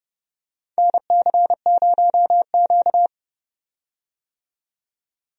Morse code, heard as a keyed tone, about 31 wpm, 700 Hz. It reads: NC0Q
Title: USA Call Signs - 4 Characters - 30wpm